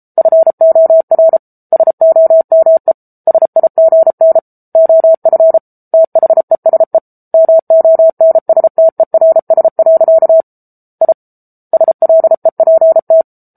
Morse code, heard as a keyed tone, about 34 wpm, 650 Hz. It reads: FOR SOME SIGN OF THESE MONSTERS. I SLEPT